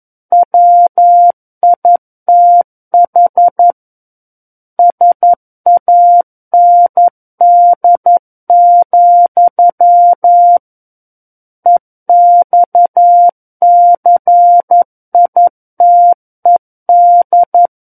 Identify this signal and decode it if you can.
Morse code, heard as a keyed tone, about 11 wpm, 700 Hz. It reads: WITH SAND, EXCITED
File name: 11 WPM